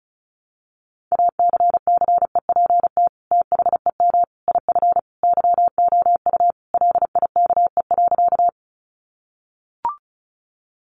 Morse code, heard as a keyed tone, about 34 wpm, 700 Hz. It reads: ACCEPT THEM IF YOU LIKE.
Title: Sentences from Top 1300 Words 35wpm